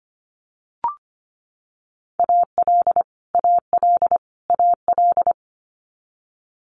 Morse code, heard as a keyed tone, about 25 wpm, 700 Hz. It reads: AL AL AL
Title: Multi-letter Phonemes Repeated 3x 25wpm